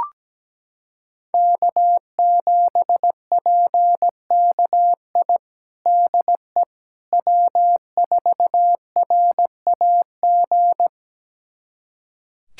Morse code, heard as a keyed tone, about 17 wpm, 700 Hz. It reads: K7PKI DE W4RAG
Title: USA Callsign DE Callsign - 5 Characters - 17wpm